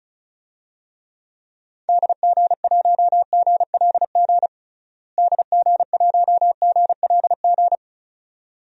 Morse code, heard as a keyed tone, about 36 wpm, 700 Hz. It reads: DG1GLG DG1GLG
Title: DXCC-100 Call Signs - 6 Characters - Repeated 2x - 35wpm